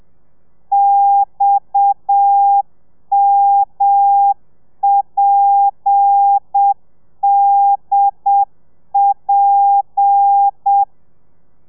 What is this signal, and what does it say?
Morse code, heard as a keyed tone, about 7 wpm, 800 Hz. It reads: XMPDP
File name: telegrafia / i2tfj / Lez7 30